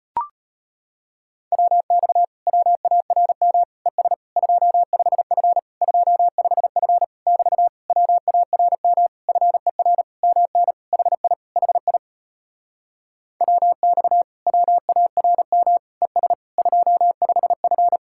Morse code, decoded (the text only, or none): WX WARM ES 25F 25F <BT> WARM FER MN HI HI WX WARM ES 25F